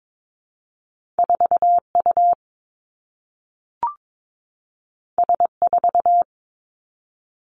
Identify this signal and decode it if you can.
Morse code, heard as a keyed tone, about 22 wpm, 700 Hz. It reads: 4 U S 4
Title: Two Challenging Letter-Number Combinations 22wpm